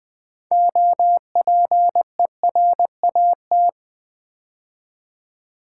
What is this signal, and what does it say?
Morse code, heard as a keyed tone, about 20 wpm, 700 Hz. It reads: OPERAT
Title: Build A Word - 3 Letters At A Time - 9-Letter Words - 20wpm